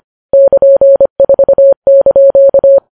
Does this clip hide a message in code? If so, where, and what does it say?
Yes: Morse code from 0.3 to 2.8 s, heard as a keyed tone, about 25 wpm, 550 Hz. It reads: <KN>4)